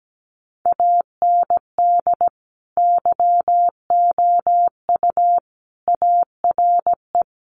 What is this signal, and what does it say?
Morse code, heard as a keyed tone, about 17 wpm, 700 Hz. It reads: AND YOU ARE
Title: Sentences from Top 1100 Words 17wpm